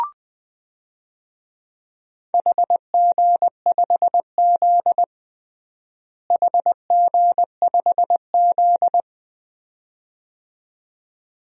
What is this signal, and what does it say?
Morse code, heard as a keyed tone, about 20 wpm, 700 Hz. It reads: HG5Z HG5Z
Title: DXCC-100 Call Signs - 4 Characters - Repeated 2x - 20wpm